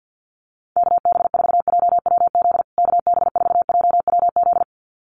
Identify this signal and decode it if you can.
Morse code, heard as a keyed tone, about 50 wpm, 700 Hz. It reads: KB4JWZ KB4JWZ
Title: USA Call Signs Repeated 2x 50wpm